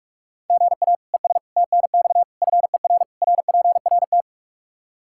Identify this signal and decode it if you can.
Morse code, heard as a keyed tone, about 46 wpm, 700 Hz. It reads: GA ES TNX FER RPRT
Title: Practice QSO - Exchange 1 - Beginning - 45wpm